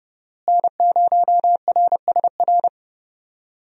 Morse code, heard as a keyed tone, about 31 wpm, 700 Hz. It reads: N0RSR
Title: USA Call Signs - 5 Characters - Repeated 2x - 30wpm